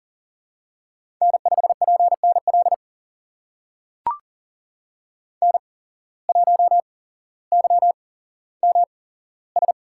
Morse code, heard as a keyed tone, about 40 wpm, spaced as for 17 wpm, 700 Hz. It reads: N5PNL N1YMS
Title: USA Call Signs - 5 Characters - 8x Character Spacing - 40wpm